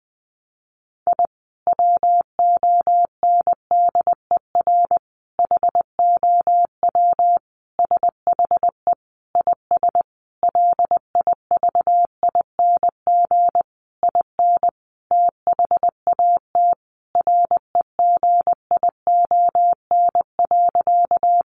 Morse code, heard as a keyed tone, about 20 wpm, 700 Hz. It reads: I WONDER HOW SHE IS LIVING IN THAT REGION.
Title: Sentences from Top 1500 Words 20wpm